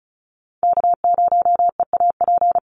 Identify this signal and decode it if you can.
Morse code, heard as a keyed tone, about 34 wpm, 700 Hz. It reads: K0EAP
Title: NA SOTA Chaser Callsigns 35wpm